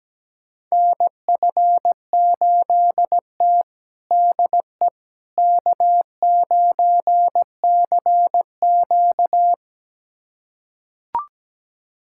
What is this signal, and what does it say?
Morse code, heard as a keyed tone, about 17 wpm, 700 Hz. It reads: NF8T DE K9CQ
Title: USA Callsign DE Callsign - 4 Characters - 17wpm